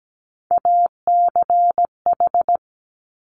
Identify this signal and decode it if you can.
Morse code, heard as a keyed tone, about 17 wpm, 690 Hz. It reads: ACH
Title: Top Three Letter Combinations in Words 17wpm